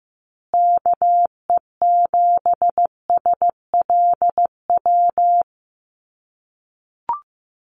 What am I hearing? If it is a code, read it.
Morse code, heard as a keyed tone, about 15 wpm, 700 Hz. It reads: KE7SLW
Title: USA Call Signs Repeated 2x 15wpm